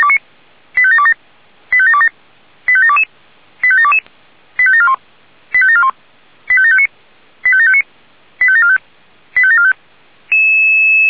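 Five-tone selective call (ZVEI), heard as quick runs of five tones, tones of about 70 ms. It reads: 85727 85727 85722 85722 85741 85741 85759 85759 85745 85745
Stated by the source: ZVEI five-tone call received on SDR.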